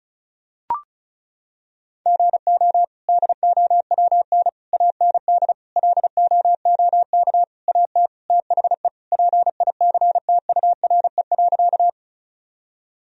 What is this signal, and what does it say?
Morse code, heard as a keyed tone, about 36 wpm, 700 Hz. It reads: GO DOWN AND LOOK AT THE PICTURE.
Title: Sentences from Top 1000 Words - Review Part 1 - 35wpm